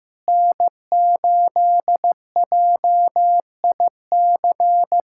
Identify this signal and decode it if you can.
Morse code, heard as a keyed tone, about 15 wpm, 700 Hz. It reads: N8JIC
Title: USA Callsign DE Callsign - 5 Characters - 15wpm